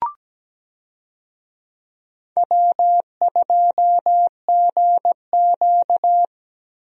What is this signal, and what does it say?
Morse code, heard as a keyed tone, about 17 wpm, 700 Hz. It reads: W2GQ
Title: USA Call Signs - 4 Characters - Repeated 3x - 17wpm